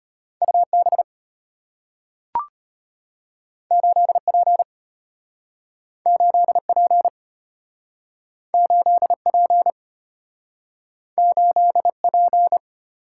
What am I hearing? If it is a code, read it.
Morse code, heard as a keyed tone, about 34 wpm, 700 Hz. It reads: UB 8P 8P 8P 8P